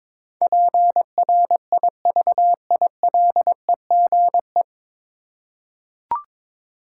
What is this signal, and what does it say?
Morse code, heard as a keyed tone, about 22 wpm, 700 Hz. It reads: PRIVILEGE
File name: 9 Letter Words - First 3 Letters Given - 22wpm